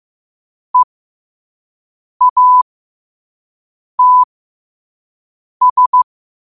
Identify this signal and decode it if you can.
Morse code, heard as a keyed tone, about 15 wpm, spaced as for 5 wpm, 1000 Hz. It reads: EATS
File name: qotdfiles / 5wpm